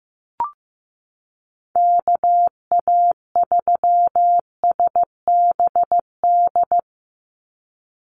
Morse code, heard as a keyed tone, about 15 wpm, 700 Hz. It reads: KA3SBD
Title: USA Call Signs 15wpm